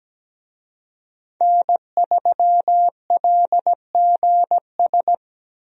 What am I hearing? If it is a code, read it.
Morse code, heard as a keyed tone, about 17 wpm, 700 Hz. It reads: N3LGS
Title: USA Call Signs - 5 Characters - Repeated 3x - 17wpm